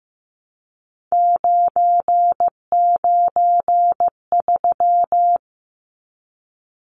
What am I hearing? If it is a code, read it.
Morse code, heard as a keyed tone, about 15 wpm, 700 Hz. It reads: 993
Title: Numbers - 3-Digit 15wpm